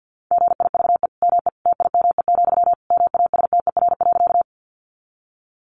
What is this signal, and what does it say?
Morse code, heard as a keyed tone, about 50 wpm, 700 Hz. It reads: GIVE ME TIME, MASTER.
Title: Sentences from Top 1300 Words 50wpm